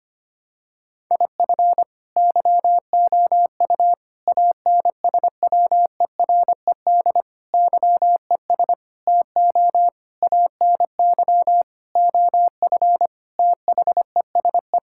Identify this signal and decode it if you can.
Morse code, heard as a keyed tone, about 25 wpm, 700 Hz. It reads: IF YOU ANSWERED YES TO ANY OF THESE
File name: Sets of 9 Words - 25wpm